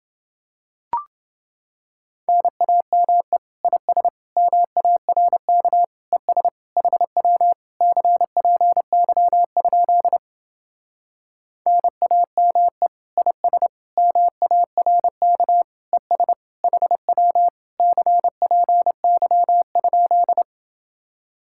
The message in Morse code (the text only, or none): NAME IS MARK ES HW CPY? NAME IS MARK ES HW CPY?